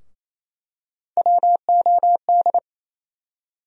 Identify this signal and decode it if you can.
Morse code, heard as a keyed tone, about 28 wpm, 700 Hz. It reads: WOD
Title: USA Call Sign Postfixes - 3 Letters 28wpm